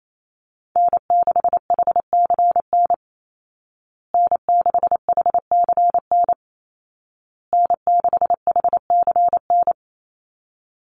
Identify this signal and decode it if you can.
Morse code, heard as a keyed tone, about 28 wpm, 700 Hz. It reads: N6HCN N6HCN N6HCN
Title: Call Signs Repeated 2x - 28wpm